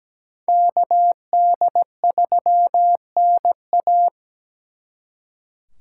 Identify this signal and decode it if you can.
Morse code, heard as a keyed tone, about 20 wpm, 700 Hz. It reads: KD3NA
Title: Speed Racing - USA 2x2 Call Signs 17wpm